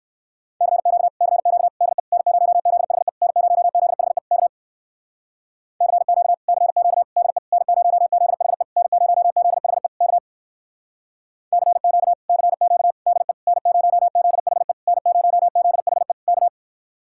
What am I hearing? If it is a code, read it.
Morse code, heard as a keyed tone, about 59 wpm, 700 Hz. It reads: CQ CQ DE N0ZHE N0ZHE K CQ CQ DE N0ZHE N0ZHE K CQ CQ DE N0ZHE N0ZHE K